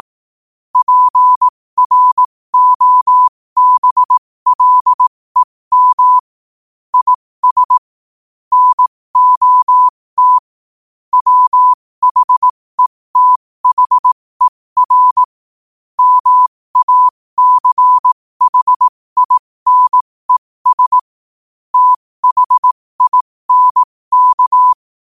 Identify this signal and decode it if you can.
Morse code, heard as a keyed tone, about 18 wpm, spaced as for 15 wpm, 1000 Hz. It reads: PROBLEM IS NOT WHETHER MACHINES THINK